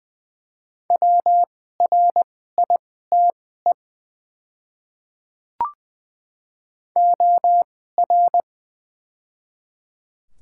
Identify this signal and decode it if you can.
Morse code, heard as a keyed tone, about 20 wpm, spaced as for 15 wpm, 700 Hz. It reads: WRITE OR
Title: 2x Character Spacing - Top 100 Words - 20wpm